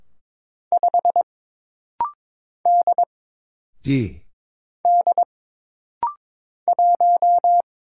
Morse code, heard as a keyed tone, about 23 wpm, 700 Hz. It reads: 5 D D 1
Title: Single Letter-Number - Rapid-Fire 22wpm